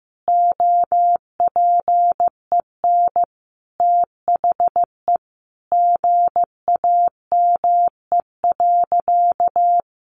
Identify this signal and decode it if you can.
Morse code, heard as a keyed tone, about 15 wpm, 700 Hz. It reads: OPEN THE GAME.
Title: Sentences from Top 1500 Words 15wpm